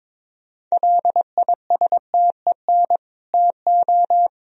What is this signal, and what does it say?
Morse code, heard as a keyed tone, about 22 wpm, 700 Hz. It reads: LISTEN TO
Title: Sentences from Top 400 Words and Abbreviations 22wpm